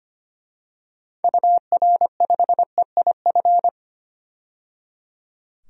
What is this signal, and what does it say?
Morse code, heard as a keyed tone, about 25 wpm, 690 Hz. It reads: UR5EIF
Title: DXCC-100 Call Signs - 6 Characters - 25wpm